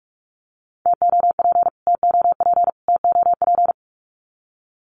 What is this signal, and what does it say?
Morse code, heard as a keyed tone, about 44 wpm, 700 Hz. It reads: TOP TOP TOP
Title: Top 500 Words Repeated 45wpm